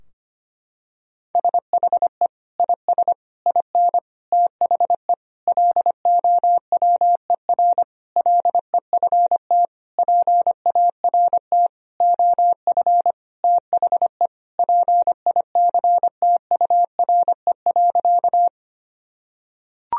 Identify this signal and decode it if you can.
Morse code, heard as a keyed tone, about 25 wpm, 700 Hz. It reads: SHE IS IN THE LOWER LEFT PART OF THE PICTURE.
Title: Sentences from Top 1000 Words - Review Part 4 - 25wpm